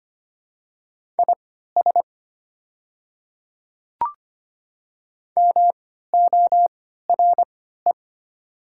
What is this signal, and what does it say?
Morse code, heard as a keyed tone, about 25 wpm, spaced as for 14 wpm, 700 Hz. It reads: IS MORE
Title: 3x Character Spacing - Top 200 Words - 25wpm